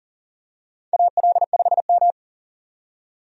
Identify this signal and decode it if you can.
Morse code, heard as a keyed tone, about 40 wpm, 700 Hz. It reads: AL5M
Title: USA Call Signs - 4 Characters - 40wpm